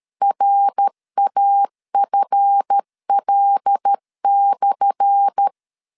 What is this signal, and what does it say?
Morse code, heard as a keyed tone, about 13 wpm, 800 Hz. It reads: RAFL/
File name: LEZIONI / 10 SIGNS / 10A SLASH 12wpm 2314